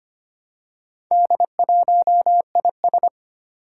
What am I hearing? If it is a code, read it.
Morse code, heard as a keyed tone, about 25 wpm, 700 Hz. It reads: D1IS
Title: Four Challenging Letter-Number Combinations 25wpm